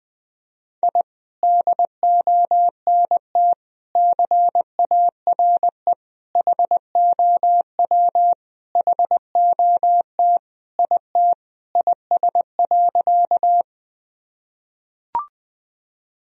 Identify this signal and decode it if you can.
Morse code, heard as a keyed tone, about 20 wpm, 700 Hz. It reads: I DONT CARE HOW HOT IT IS.
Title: Sentences from Top 1000 Words - Review Part 3 - 20wpm